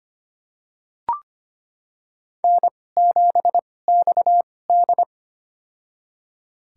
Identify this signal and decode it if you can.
Morse code, heard as a keyed tone, about 25 wpm, spaced as for 18 wpm, 700 Hz. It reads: N7XD
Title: USA Call Signs - 4 Characters - 2x Character Spacing - 25wpm